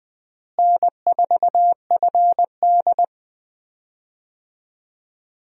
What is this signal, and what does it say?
Morse code, heard as a keyed tone, about 20 wpm, 700 Hz. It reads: N4FD
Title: USA Call Signs - 4 Characters - Repeated 2x - 20wpm